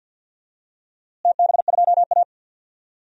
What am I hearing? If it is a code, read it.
Morse code, heard as a keyed tone, about 53 wpm, 700 Hz. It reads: TB2A
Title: DXCC-100 Call Signs - 4 Characters - Repeated 3x - 50wpm